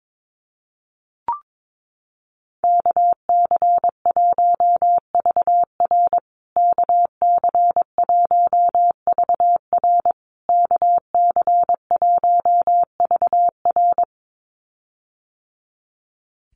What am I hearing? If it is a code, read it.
Morse code, heard as a keyed tone, about 22 wpm, 700 Hz. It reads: KC1VR KC1VR KC1VR
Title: USA Call Signs Repeated 3x 22wpm